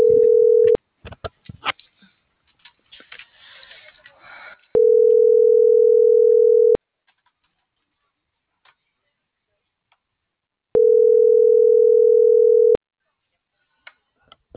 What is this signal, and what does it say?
Ringback tone (North America), 440 + 480 Hz.